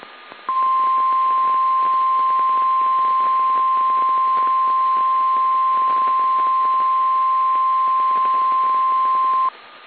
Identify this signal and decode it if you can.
Emergency Alert System (SAME) tones (the 1050 Hz attention tone). It is heard from 0.5 to 9.5 s.